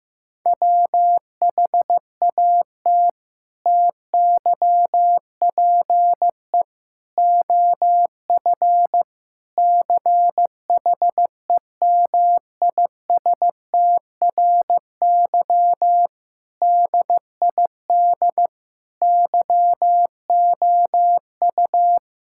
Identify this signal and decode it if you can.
Morse code, heard as a keyed tone, about 15 wpm, 700 Hz. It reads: WHAT TYPE OF CHEMISTRY DID YOU
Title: Sentences from Top 1000 Words - Review Part 6 - 15wpm